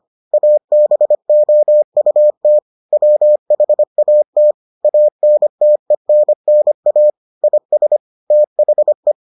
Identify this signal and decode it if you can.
Morse code, heard as a keyed tone, about 25 wpm, 600 Hz. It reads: ABOUT WHAT ANTENNA IS THE